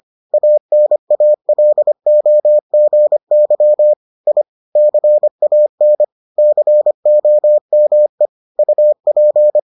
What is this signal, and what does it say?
Morse code, heard as a keyed tone, about 25 wpm, 600 Hz. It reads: ANALOGY I CAN COME UP